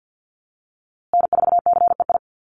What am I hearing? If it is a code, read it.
Morse code, heard as a keyed tone, about 50 wpm, 700 Hz. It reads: N4CEI
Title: USA Call Signs Repeated 2x 50wpm